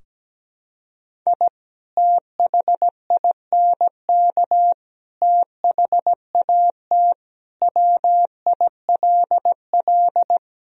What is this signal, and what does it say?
Morse code, heard as a keyed tone, about 17 wpm, 700 Hz. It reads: I THINK THAT WILL
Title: Sentences from Top 1000 Words - Review Part 2 - 17wpm